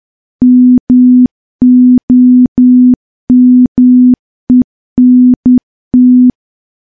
Morse code, heard as a keyed tone, about 10 wpm, 250 Hz. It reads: MOMENT
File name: morse / morse-training-10WPM